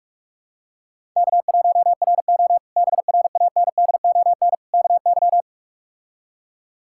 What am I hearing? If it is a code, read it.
Morse code, heard as a keyed tone, about 42 wpm, 700 Hz. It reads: K1RO BRANDON KY
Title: K1USN SST QSO Practice - 45wpm